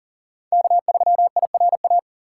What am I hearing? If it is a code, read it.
Morse code, heard as a keyed tone, about 38 wpm, 700 Hz. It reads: K3IRA
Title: USA Call Signs - 5 Characters - 3x Character Spacing - 40wpm